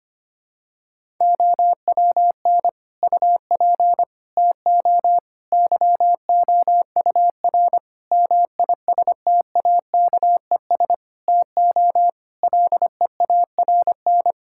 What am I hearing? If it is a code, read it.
Morse code, heard as a keyed tone, about 25 wpm, 700 Hz. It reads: OWN UP TO YOUR MISTAKES TO LEARN
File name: practice / 25wpm / Difficult Sentences with Hints - 25wpm